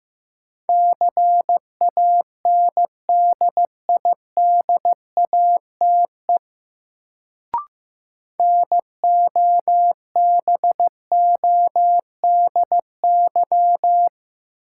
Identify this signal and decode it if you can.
Morse code, heard as a keyed tone, about 15 wpm, 700 Hz. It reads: CANDIDATE NOBODY
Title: Sentences from Top 1600 Words - New Word Drill - 15wpm